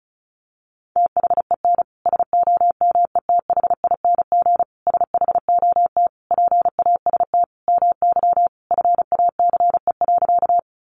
Morse code, heard as a keyed tone, about 34 wpm, 700 Hz. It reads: THEN SOMETHING SHOT PAST MY FACE.
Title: Sentences from Top 1100 Words 35wpm